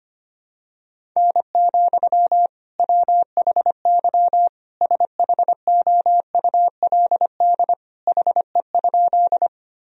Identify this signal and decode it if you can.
Morse code, heard as a keyed tone, about 25 wpm, 700 Hz. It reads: N, WHY SHOULD HE?
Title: Sentences from Top 300 Words and Abbreviations 25wpm